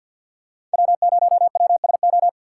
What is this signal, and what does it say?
Morse code, heard as a keyed tone, about 47 wpm, 700 Hz. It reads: W0WSO
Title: USA Call Signs - 5 Characters - 4x Character Spacing - 50wpm